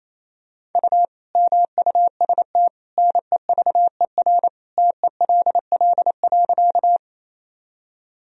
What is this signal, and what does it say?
Morse code, heard as a keyed tone, about 28 wpm, 700 Hz. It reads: U MUST NEVER TELL.
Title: Sentences from Top 300 Words and Abbreviations 28wpm